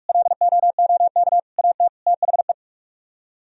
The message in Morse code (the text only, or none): LOOK AT THE